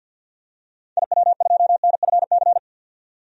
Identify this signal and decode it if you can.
Morse code, heard as a keyed tone, about 45 wpm, 700 Hz. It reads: IW2NFC
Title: DXCC-100 Call Signs - 6 Characters - 50wpm